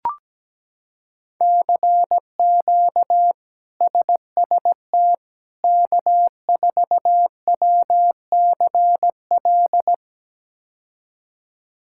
Morse code, heard as a keyed tone, about 17 wpm, 700 Hz. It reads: CQ SST K4WCL
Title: K1USN SST QSO Practice - 17wpm